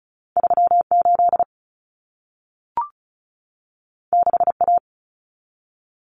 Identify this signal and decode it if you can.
Morse code, heard as a keyed tone, about 35 wpm, 700 Hz. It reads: 38 6A
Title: Common Cut Numbers - 2-Digit 35wpm